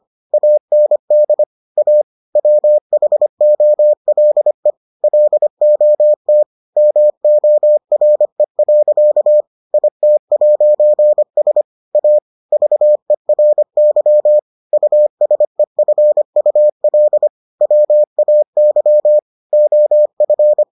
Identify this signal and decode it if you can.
Morse code, heard as a keyed tone, about 25 wpm, 600 Hz. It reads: AND A WHOLE LOT MORE. IT'S A VERY USEFUL WAY OF